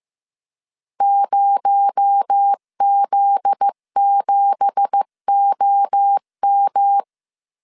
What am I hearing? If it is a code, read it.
Morse code, heard as a keyed tone, about 15 wpm, 800 Hz. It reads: 0Z7OM
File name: CW / DET / 1 9 1500C S B 14wpm 3302